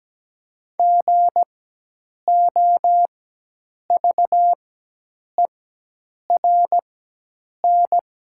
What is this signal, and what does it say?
Morse code, heard as a keyed tone, about 17 wpm, spaced as for 8 wpm, 700 Hz. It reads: GOVERN
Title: 4x Character Spacing - Top 500 Words - 17wpm